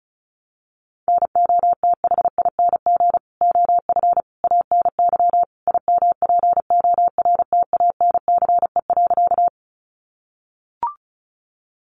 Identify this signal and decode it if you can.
Morse code, heard as a keyed tone, about 34 wpm, 700 Hz. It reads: NOTHING OF ANY IMPORTANCE.
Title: Sentences from Top 1200 Words 35wpm